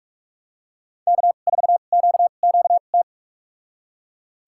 Morse code, heard as a keyed tone, about 46 wpm, spaced as for 32 wpm, 700 Hz. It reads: K4QQT
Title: USA Call Signs - 5 Characters - 2x Character Spacing - 45wpm